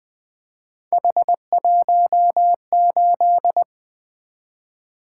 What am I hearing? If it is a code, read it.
Morse code, heard as a keyed tone, about 20 wpm, 700 Hz. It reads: H18
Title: Three Letter-Number Combinations - 20wpm